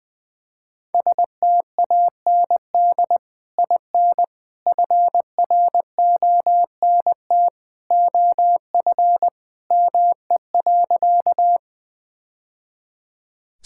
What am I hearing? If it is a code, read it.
Morse code, heard as a keyed tone, about 20 wpm, 700 Hz. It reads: STAND IN FRONT OF ME.
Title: Sentences with 2-Word Phrases - 20wpm